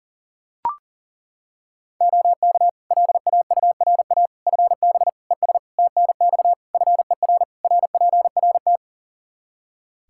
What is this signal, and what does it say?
Morse code, heard as a keyed tone, about 40 wpm, 700 Hz. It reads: OK LAURA FB ES TNX FER RPRT
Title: K1USN SST QSO Practice 40wpm